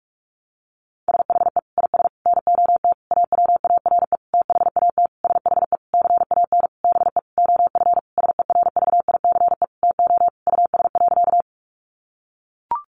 Morse code, heard as a keyed tone, about 44 wpm, 700 Hz. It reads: SHE IS NOT AWARE THAT SHE CAN BE OF SERVICE TO US.
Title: Sentences from Top 1200 Words 45wpm